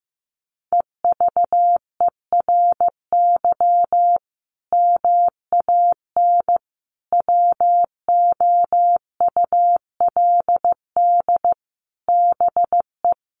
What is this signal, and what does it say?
Morse code, heard as a keyed tone, about 15 wpm, 700 Hz. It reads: EVERY MAN WOULD BE
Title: Sentences from Top 1500 Words 15wpm